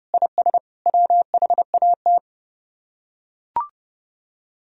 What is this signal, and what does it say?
Morse code, heard as a keyed tone, about 32 wpm, 700 Hz. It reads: IS WHAT